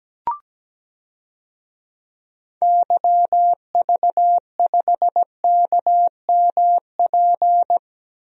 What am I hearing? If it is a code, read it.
Morse code, heard as a keyed tone, about 17 wpm, 700 Hz. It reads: YV5KMP
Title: DXCC-100 Call Signs - 6 Characters - Repeated 2x - 17wpm